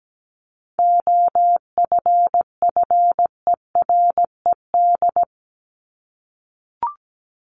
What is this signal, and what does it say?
Morse code, heard as a keyed tone, about 17 wpm, 700 Hz. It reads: OFFERED
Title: ED Suffix - Top 100 Words with Base Words 17wpm